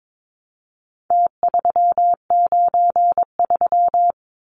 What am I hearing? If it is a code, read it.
Morse code, heard as a keyed tone, about 22 wpm, 700 Hz. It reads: T393
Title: Numbers - Contest Style - 4-Digit 22wpm